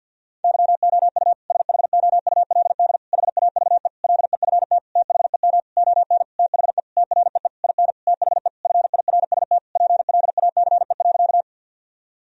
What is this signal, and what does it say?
Morse code, heard as a keyed tone, about 51 wpm, 700 Hz. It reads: YOU SHOULD HAVE LEFT THEM ON THE TREE IN THE FIRST PLACE.
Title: Sentences from Top 1000 Words - Review Part 1 - 50wpm